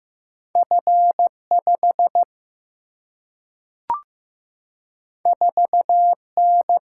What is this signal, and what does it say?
Morse code, heard as a keyed tone, about 15 wpm, 700 Hz. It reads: F 5 4 N
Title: Single-Letter Single-Number 15wpm